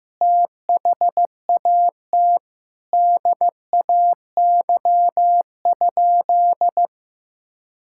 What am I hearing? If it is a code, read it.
Morse code, heard as a keyed tone, about 15 wpm, 700 Hz. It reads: THAT DAY?
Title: Sentences from Top 1600 Words 15wpm